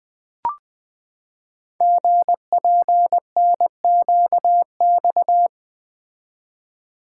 Morse code, heard as a keyed tone, about 20 wpm, 700 Hz. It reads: GPNQX
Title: Code Groups - 5 Letter:Numbers - 20wpm